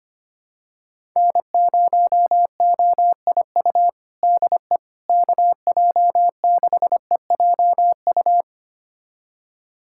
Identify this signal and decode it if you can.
Morse code, heard as a keyed tone, about 25 wpm, 700 Hz. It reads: N0OIU DE KJ6EJU
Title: USA Callsign DE Callsign 25wpm